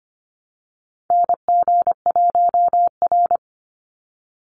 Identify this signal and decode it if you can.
Morse code, heard as a keyed tone, about 25 wpm, 700 Hz. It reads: NG1R
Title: Call Signs Repeated 2x - 25wpm